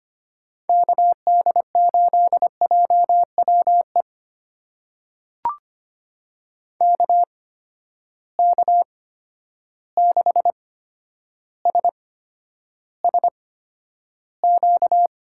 Morse code, heard as a keyed tone, about 25 wpm, spaced as for 11 wpm, 700 Hz. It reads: KD8JWE KK6SSQ
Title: USA Call Signs - 6 Characters - 8x Character Spacing - 25wpm